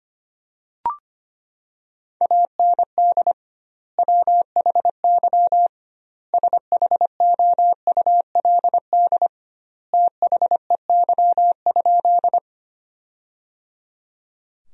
Morse code, heard as a keyed tone, about 25 wpm, 700 Hz. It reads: AND WHY SHOULD THEY?
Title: 2x Word-Spacing - Sentences from Top 300 Words - 25wpm